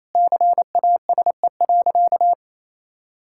Morse code, heard as a keyed tone, about 28 wpm, 700 Hz. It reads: CASE.
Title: Sentences from Top 1000 Words - Review Part 6 - 28wpm